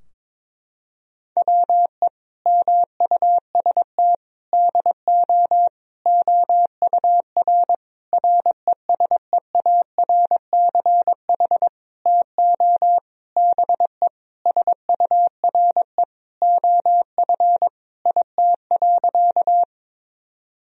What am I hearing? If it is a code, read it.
Morse code, heard as a keyed tone, about 22 wpm, 700 Hz. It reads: WE MUST DO OUR RESEARCH TO BE SURE OF IT.
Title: Sentences from Top 1000 Words - Review Part 4 - 22wpm